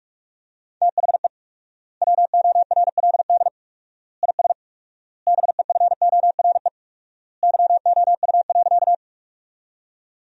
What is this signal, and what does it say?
Morse code, heard as a keyed tone, about 45 wpm, 700 Hz. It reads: THE WORLD IS BEFORE YOU.
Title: 4x Word-Spacing - Sentences from Top 200 Words - 45wpm